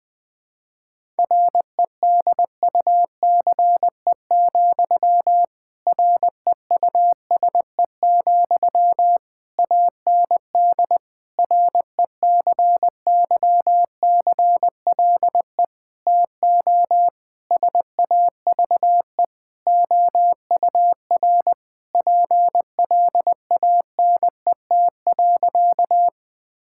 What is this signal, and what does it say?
Morse code, heard as a keyed tone, about 21 wpm, 700 Hz. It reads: REDUCE, REUSE, AND RECYCLE TO SAVE OUR PLANET.
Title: Difficult Sentences with Hints - 20wpm